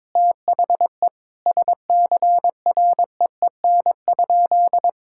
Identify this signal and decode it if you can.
Morse code, heard as a keyed tone, about 22 wpm, 700 Hz. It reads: THE SCREEN?
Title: Sentences from Top 1000 Words - Review Part 6 - 22wpm